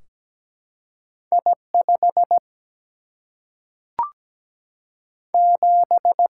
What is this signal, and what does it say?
Morse code, heard as a keyed tone, about 17 wpm, 700 Hz. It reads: I 5 7
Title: Single-Letter Single-Number 17wpm